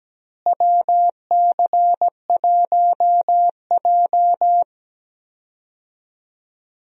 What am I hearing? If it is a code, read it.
Morse code, heard as a keyed tone, about 17 wpm, 700 Hz. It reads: WC1J
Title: USA Call Signs - 4 Characters - Repeated 3x - 17wpm